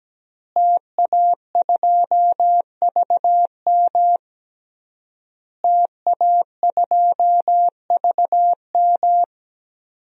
Morse code, heard as a keyed tone, about 17 wpm, 700 Hz. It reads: TA2VM TA2VM
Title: DXCC-100 Call Signs - 5 Characters - Repeated 3x - 17wpm